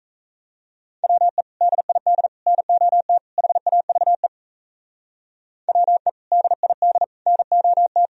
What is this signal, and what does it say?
Morse code, heard as a keyed tone, about 36 wpm, 700 Hz. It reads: WE DID NOT HAVE WE DID NOT